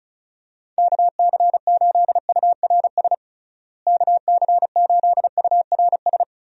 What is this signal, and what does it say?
Morse code, heard as a keyed tone, about 34 wpm, 700 Hz. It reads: KC8URS KC8URS
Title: USA Call Signs - 6 Characters - Repeated 3x - 35wpm